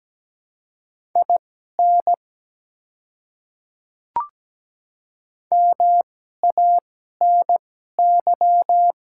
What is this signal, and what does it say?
Morse code, heard as a keyed tone, about 17 wpm, spaced as for 12 wpm, 700 Hz. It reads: IN MANY
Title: 2x Character Spacing - Top 200 Words - 17wpm